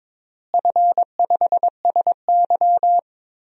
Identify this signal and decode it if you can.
Morse code, heard as a keyed tone, about 22 wpm, 700 Hz. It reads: F5SY
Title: DXCC-100 Call Signs - 4 Characters - Repeated 3x - 22wpm